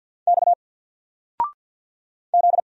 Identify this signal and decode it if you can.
Morse code, heard as a keyed tone, about 50 wpm, 700 Hz. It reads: X Z
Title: Single Letter-Number 50wpm